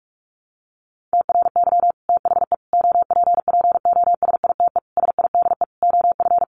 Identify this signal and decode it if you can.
Morse code, heard as a keyed tone, about 44 wpm, 700 Hz. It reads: TRY THE OPPOSITE SIDE OF
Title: Sentences from Top 1300 Words 45wpm